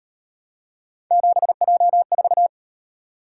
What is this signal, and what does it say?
Morse code, heard as a keyed tone, about 38 wpm, 700 Hz. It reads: 7J4